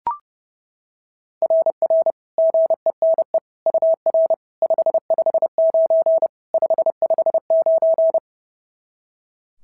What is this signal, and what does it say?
Morse code, heard as a keyed tone, about 30 wpm, 690 Hz. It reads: RR GENE UR 559 559
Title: SOTA QSO Practice 30wpm